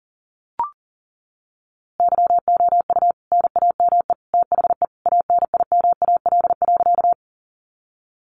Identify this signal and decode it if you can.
Morse code, heard as a keyed tone, about 40 wpm, 700 Hz. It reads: YOU NAME THE ANIMAL.
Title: A Sound Of Thunder 40wpm